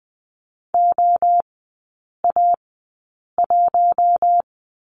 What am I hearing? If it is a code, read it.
Morse code, heard as a keyed tone, about 20 wpm, spaced as for 8 wpm, 700 Hz. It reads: OA1
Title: 060 Lesson 11e - Letter h - Three Characters 20wpm